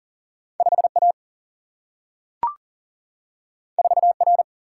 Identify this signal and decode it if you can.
Morse code, heard as a keyed tone, about 39 wpm, 700 Hz. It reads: 5 A 4 R
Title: Single-Letter Single-Number 40wpm